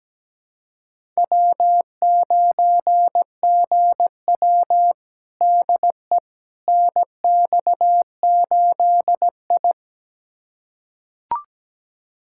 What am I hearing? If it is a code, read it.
Morse code, heard as a keyed tone, about 17 wpm, 700 Hz. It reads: W9GW DE NX8I
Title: USA Callsign DE Callsign - 4 Characters - 17wpm